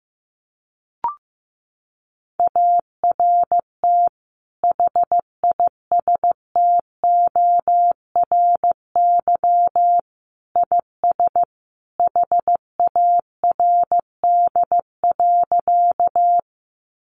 Morse code, heard as a keyed tone, about 15 wpm, 700 Hz. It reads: ART HISTORY IS HARD.
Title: Sentences from Top 600 Words 15wpm